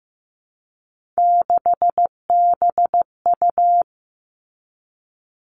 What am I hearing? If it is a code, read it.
Morse code, heard as a keyed tone, about 15 wpm, 700 Hz. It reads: 6BU
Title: Three Challenging Letter-Number Combinations 15wpm